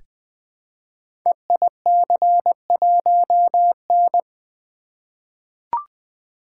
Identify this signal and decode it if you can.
Morse code, heard as a keyed tone, about 20 wpm, 700 Hz. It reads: EIC1N
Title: Code Groups - 5 Letter:Numbers - 20wpm